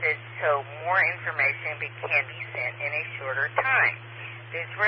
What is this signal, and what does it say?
Emergency Alert System (SAME) tones, heard as harsh buzzing bursts, with speech. It is heard from 1.0 to 4.0 s.